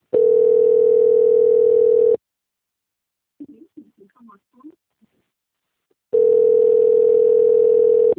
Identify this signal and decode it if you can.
Ringback tone (North America), 440 + 480 Hz.